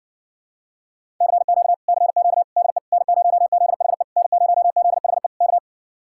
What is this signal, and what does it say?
Morse code, heard as a keyed tone, about 69 wpm, 700 Hz. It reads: CQ CQ DE N0ZHE N0ZHE K